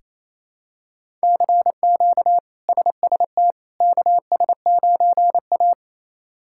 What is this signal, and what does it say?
Morse code, heard as a keyed tone, about 28 wpm, 700 Hz. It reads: CQ SST KS9A
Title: K1USN SST QSO Practice - 28wpm